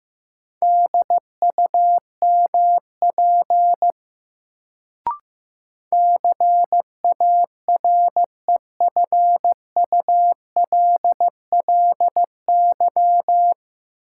Morse code, heard as a keyed tone, about 15 wpm, 700 Hz. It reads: DUMP CAREFULLY
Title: Sentences from Top 1300 Words - New Word Drill Alt - 15wpm